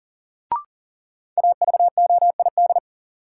Morse code, heard as a keyed tone, about 40 wpm, 700 Hz. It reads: AVOID
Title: Sentences from Top 1000 Words - Review Part 6 - New Word Drill - 40wpm